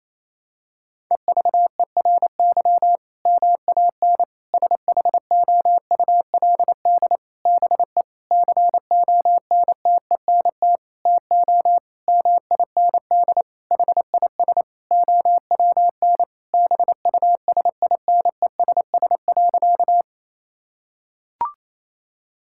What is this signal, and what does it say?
Morse code, heard as a keyed tone, about 28 wpm, 700 Hz. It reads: EVERY MAN SHOULD BE CONTENT TO MIND HIS OWN BUSINESS.
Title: Sentences from Top 1000 Words - Review Part 6 - 28wpm